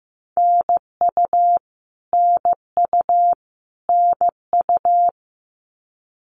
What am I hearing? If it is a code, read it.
Morse code, heard as a keyed tone, about 15 wpm, 700 Hz. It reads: NU NU NU
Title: Canadian Provinces and Territories - Repeated 3x - 15wpm